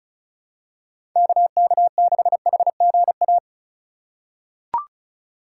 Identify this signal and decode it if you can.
Morse code, heard as a keyed tone, about 36 wpm, 700 Hz. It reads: KK6HGA
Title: USA Call Signs - 3x Character Spacing - 35wpm